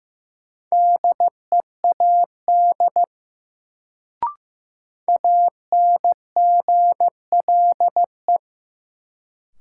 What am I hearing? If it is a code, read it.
Morse code, heard as a keyed tone, about 15 wpm, 700 Hz. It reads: DEAD ANGLE
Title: Sentences from Top 1500 Words - New Word Drill - 15wpm